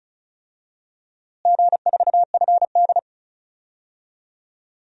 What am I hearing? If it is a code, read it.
Morse code, heard as a keyed tone, about 34 wpm, 700 Hz. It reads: G4FD
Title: DXCC-100 Call Signs - 4 Characters - Repeated 3x - 35wpm